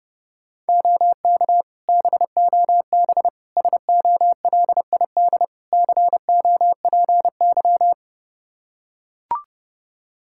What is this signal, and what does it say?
Morse code, heard as a keyed tone, about 32 wpm, 700 Hz. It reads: OK BOB SOLID COPY
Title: Practice QSO - Exchange 3 - More Info - 30wpm